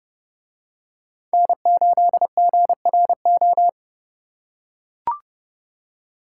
Morse code, heard as a keyed tone, about 29 wpm, 700 Hz. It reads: N8GRO